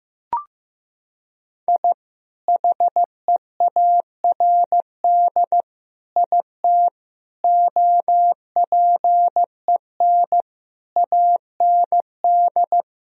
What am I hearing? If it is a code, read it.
Morse code, heard as a keyed tone, about 15 wpm, 700 Hz. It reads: I HEARD IT OPEN AND
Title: Sentences from Top 1000 Words - Review Part 1 - 15wpm